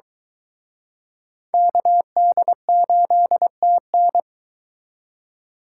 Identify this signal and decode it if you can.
Morse code, heard as a keyed tone, about 22 wpm, 700 Hz. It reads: KD8TN